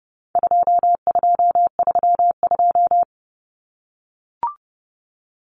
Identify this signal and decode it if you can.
Morse code, heard as a keyed tone, about 30 wpm, 700 Hz. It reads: 2232
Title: Common Cut Numbers - 4-Digit 30wpm